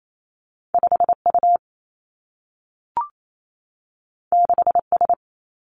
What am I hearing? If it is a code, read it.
Morse code, heard as a keyed tone, about 28 wpm, 700 Hz. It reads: 5 U 6 S
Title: Two Challenging Letter-Number Combinations 28wpm